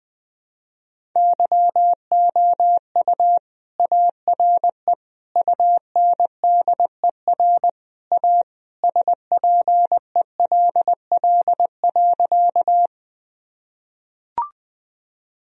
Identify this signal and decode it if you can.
Morse code, heard as a keyed tone, about 20 wpm, 700 Hz. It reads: YOU ARE UNDER A SPELL.
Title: Sentences from Top 1000 Words - Review Part 1 - 20wpm